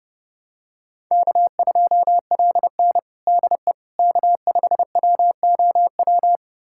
Morse code, heard as a keyed tone, about 30 wpm, 700 Hz. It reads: K2LN DE K5WOW
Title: USA Callsign DE Callsign 30wpm